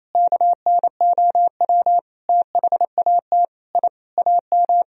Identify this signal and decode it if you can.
Morse code, heard as a keyed tone, about 27 wpm, 700 Hz. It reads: KNOW THAT I AM
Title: Sentences from Top 1000 Words - Review Part 4 - 28wpm